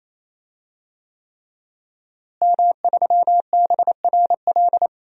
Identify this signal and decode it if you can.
Morse code, heard as a keyed tone, about 28 wpm, 700 Hz. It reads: M3BRL
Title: DXCC-100 Call Signs - 5 Characters - Repeated 3x - 28wpm